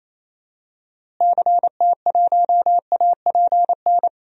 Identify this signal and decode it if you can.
Morse code, heard as a keyed tone, about 27 wpm, 700 Hz. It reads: CT1APN
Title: DXCC-100 Call Signs - 6 Characters - Repeated 2x - 28wpm